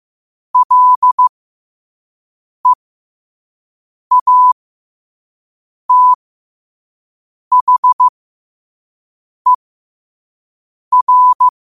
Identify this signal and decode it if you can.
Morse code, heard as a keyed tone, about 15 wpm, spaced as for 5 wpm, 1000 Hz. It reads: LEATHER
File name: qotdfiles / 5wpm